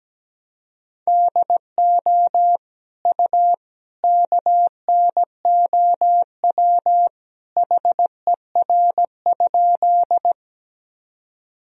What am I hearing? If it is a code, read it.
Morse code, heard as a keyed tone, about 17 wpm, 700 Hz. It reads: DO U KNOW HER?
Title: Sentences from Top 100 Words and Abbreviations 17wpm